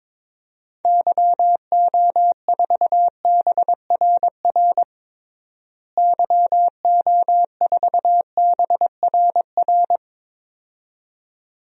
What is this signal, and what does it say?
Morse code, heard as a keyed tone, about 22 wpm, 700 Hz. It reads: YO4BRR YO4BRR
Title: DXCC-100 Call Signs - 6 Characters - Repeated 2x - 22wpm